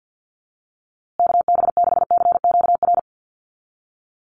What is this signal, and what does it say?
Morse code, heard as a keyed tone, about 50 wpm, 700 Hz. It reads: KB6CQR
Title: USA Call Signs Repeated 3x 50wpm